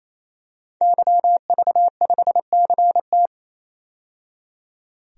Morse code, heard as a keyed tone, about 28 wpm, 700 Hz. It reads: YV5CT
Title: DXCC-100 Call Signs - 5 Characters - 28wpm